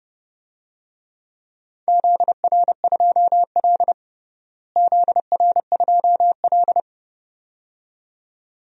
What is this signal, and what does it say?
Morse code, heard as a keyed tone, about 30 wpm, 700 Hz. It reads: ZR2L ZR2L
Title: DXCC-100 Call Signs - 4 Characters - Repeated 2x - 30wpm